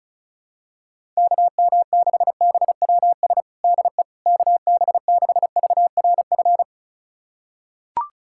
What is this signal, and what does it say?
Morse code, heard as a keyed tone, about 36 wpm, 700 Hz. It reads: KM6BWS DE KB6VRF
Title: USA Callsign DE Callsign - 6 Characters - 35wpm